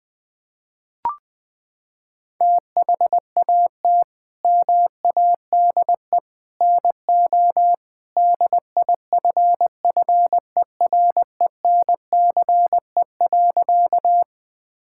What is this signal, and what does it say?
Morse code, heard as a keyed tone, about 20 wpm, 700 Hz. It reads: THAT MADE NO DIFFERENCE.
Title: Sentences from Top 1000 Words - Review Part 5 - 20wpm